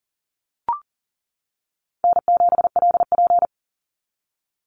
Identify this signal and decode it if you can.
Morse code, heard as a keyed tone, about 40 wpm, 700 Hz. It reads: N7LP
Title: NA SOTA Activator Callsigns 40wpm